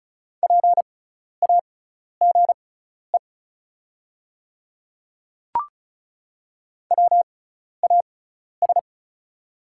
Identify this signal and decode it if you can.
Morse code, heard as a keyed tone, about 34 wpm, spaced as for 12 wpm, 690 Hz. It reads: PAGE WAS
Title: 6x Character Spacing - Top 300 Words - 35wpm